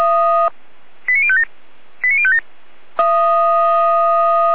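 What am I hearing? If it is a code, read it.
Five-tone selective call (ZVEI), heard as quick runs of five tones, tones of about 70 ms. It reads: 89057 89057
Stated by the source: ZVEI five-tone call received by SDR.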